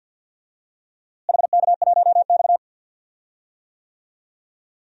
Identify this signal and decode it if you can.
Morse code, heard as a keyed tone, about 50 wpm, 700 Hz. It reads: HK1X
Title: DXCC-100 Call Signs - 4 Characters - Repeated 2x - 50wpm